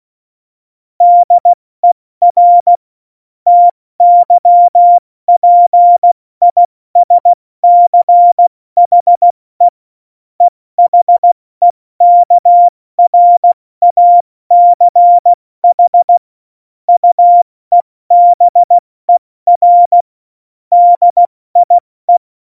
Morse code, heard as a keyed tone, about 16 wpm, 700 Hz. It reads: DER TYPISCHE EHEKRACH UEBER DIE
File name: Morsekurs / Witze / Witz 28 April 2020, 70 F80